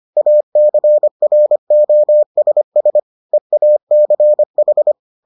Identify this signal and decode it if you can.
Morse code, heard as a keyed tone, about 25 wpm, 600 Hz. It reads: ACROSS EACH